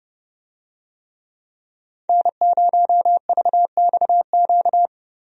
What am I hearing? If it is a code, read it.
Morse code, heard as a keyed tone, about 30 wpm, 700 Hz. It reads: N0VXQ
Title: USA Call Signs - 5 Characters - 30wpm